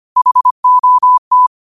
Morse code, heard as a keyed tone, about 25 wpm, 1000 Hz. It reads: HOT